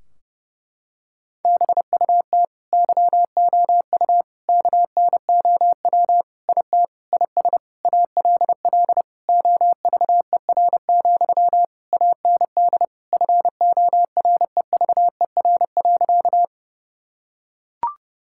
Morse code, heard as a keyed tone, about 32 wpm, 700 Hz. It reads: BUT YOU KNOW IT IS ALL OVER, AND FOREVER.
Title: Sentences from Top 1600 Words 30wpm